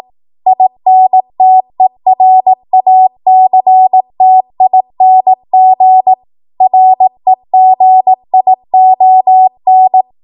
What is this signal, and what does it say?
Morse code, heard as a keyed tone, about 18 wpm, 750 Hz. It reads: INTERACTING REGION
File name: Morse / 18WPM